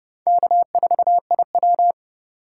Morse code, heard as a keyed tone, about 36 wpm, 700 Hz. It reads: K4IW
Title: Speed Racing - USA 1x2 Call Signs 30wpm